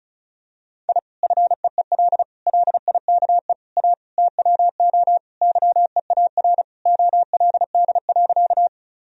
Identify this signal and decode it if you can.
Morse code, heard as a keyed tone, about 36 wpm, 700 Hz. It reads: I FEEL LIKE A TWO YEAR OLD.
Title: Sentences from Top 1000 Words - Review Part 1 - 35wpm